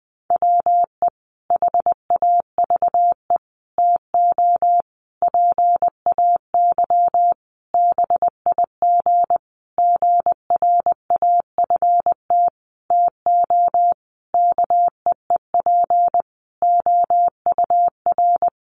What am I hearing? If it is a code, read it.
Morse code, heard as a keyed tone, about 20 wpm, 700 Hz. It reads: WE HAVE TO PAY BIG GRAFT TO KEEP OUR
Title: A Sound Of Thunder 20wpm